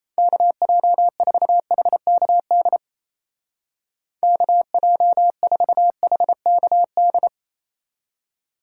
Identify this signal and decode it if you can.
Morse code, heard as a keyed tone, about 36 wpm, 700 Hz. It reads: KJ4HKD KJ4HKD